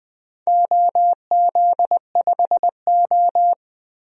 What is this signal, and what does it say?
Morse code, heard as a keyed tone, about 20 wpm, 700 Hz. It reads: OZ5O
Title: DXCC-100 Call Signs - 4 Characters - Repeated 3x - 20wpm